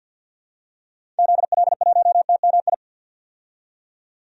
Morse code, heard as a keyed tone, about 52 wpm, 700 Hz. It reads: ZL1TMI
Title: DXCC-100 Call Signs - 6 Characters - 50wpm